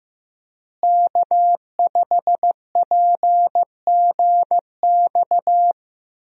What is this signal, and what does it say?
Morse code, heard as a keyed tone, about 15 wpm, 700 Hz. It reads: K5PGX
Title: USA Call Signs - 5 Characters - Repeated 2x - 15wpm